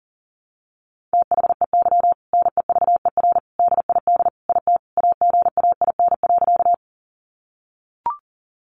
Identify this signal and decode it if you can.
Morse code, heard as a keyed tone, about 40 wpm, 700 Hz. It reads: THEY NEVER DID IT AGAIN.
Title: Sentences from Top 300 Words 40wpm